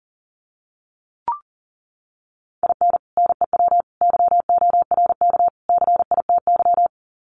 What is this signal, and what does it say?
Morse code, heard as a keyed tone, about 40 wpm, 700 Hz. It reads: IN NEW YORK CITY
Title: Sets of 4 Words 40wpm